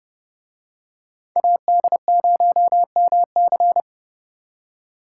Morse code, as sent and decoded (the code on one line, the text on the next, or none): .- -.. ----- -- -.-.
AD0MC